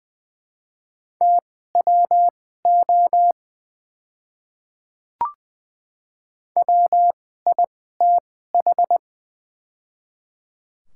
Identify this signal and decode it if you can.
Morse code, heard as a keyed tone, about 21 wpm, spaced as for 15 wpm, 700 Hz. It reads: TWO WITH
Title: 2x Character Spacing - Top 100 Words - 20wpm